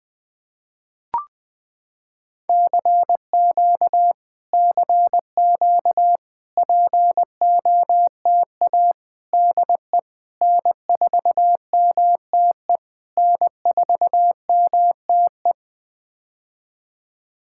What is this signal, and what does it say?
Morse code, heard as a keyed tone, about 20 wpm, 700 Hz. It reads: CQ CQ POTA DE N4MTE N4MTE
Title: POTA QSO Practice - 20wpm